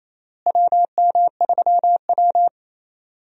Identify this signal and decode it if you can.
Morse code, heard as a keyed tone, about 28 wpm, 700 Hz. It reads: WM3W
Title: USA Call Signs - 4 Characters - Repeated 2x - 28wpm